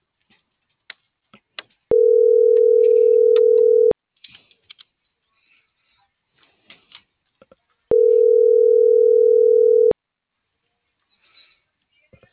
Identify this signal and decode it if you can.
Ringback tone (North America), 440 + 480 Hz.